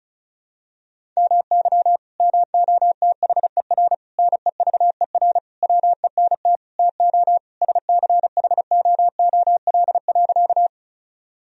Morse code, heard as a keyed tone, about 34 wpm, 700 Hz. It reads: MY MOTHER NEVER WENT TO SCHOOL.
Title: Sentences from Top 1000 Words - Review Part 1 - 35wpm